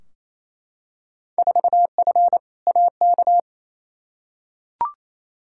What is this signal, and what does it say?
Morse code, heard as a keyed tone, about 28 wpm, 700 Hz. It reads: 4F AK
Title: Field Day Exchanges Encore - 28wpm